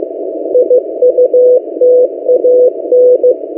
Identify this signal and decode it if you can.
Morse code, heard as a keyed tone, about 15 wpm, 500 Hz. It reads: IUTAN
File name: M08a- -1355utc-8096khz